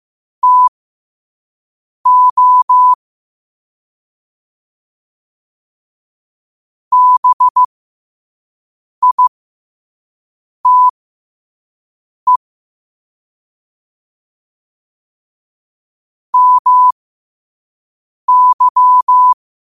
Morse code, heard as a keyed tone, about 15 wpm, spaced as for 5 wpm, 1000 Hz. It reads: TO BITE MY
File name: qotdfiles / 5wpm